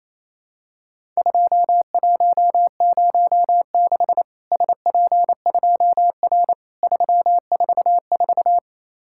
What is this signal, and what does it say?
Morse code, heard as a keyed tone, about 28 wpm, 700 Hz. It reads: 2106 SP2R 344
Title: Worked All Europe DX Contest - QTC Exchange - 28wpm